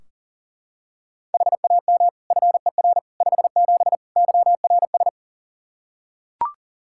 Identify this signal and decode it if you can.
Morse code, heard as a keyed tone, about 40 wpm, 700 Hz. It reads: HAM FER 57 YRS
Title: K1USN SST QSO Practice 40wpm